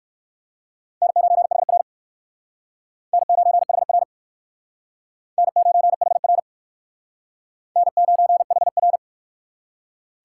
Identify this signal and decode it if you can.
Morse code, heard as a keyed tone, about 56 wpm, 700 Hz. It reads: N9HR N9HR N9HR N9HR
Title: Speed Racing - USA 1x2 Call Signs 45wpm